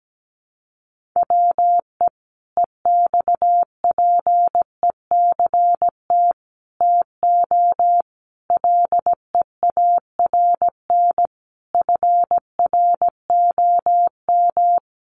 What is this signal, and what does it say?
Morse code, heard as a keyed tone, about 17 wpm, 700 Hz. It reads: WE EXPECT TO LEARN FROM
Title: Sentences from Top 1100 Words 17wpm